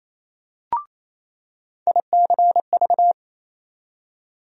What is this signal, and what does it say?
Morse code, heard as a keyed tone, about 28 wpm, 700 Hz. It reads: ICV
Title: USA Call Sign Postfixes - 3 Letters 28wpm